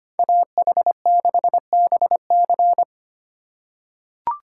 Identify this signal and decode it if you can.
Morse code, heard as a keyed tone, about 25 wpm, 700 Hz. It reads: AH6BC
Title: USA Call Signs - 5 Characters - 4x Character Spacing - 25wpm